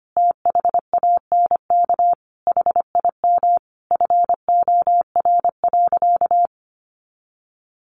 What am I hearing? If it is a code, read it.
Morse code, heard as a keyed tone, about 25 wpm, 700 Hz. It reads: THANK HIM FOR.
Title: Sentences from Top 1000 Words 25wpm